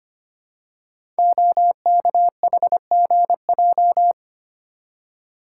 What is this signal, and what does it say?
Morse code, heard as a keyed tone, about 25 wpm, 700 Hz. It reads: OKHGJ
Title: Code Groups - 5 Letter:Numbers - 25wpm